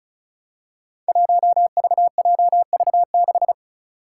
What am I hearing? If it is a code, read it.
Morse code, heard as a keyed tone, about 36 wpm, 700 Hz. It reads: 1VJV6
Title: Five Challenging Letter-Number Combinations 35wpm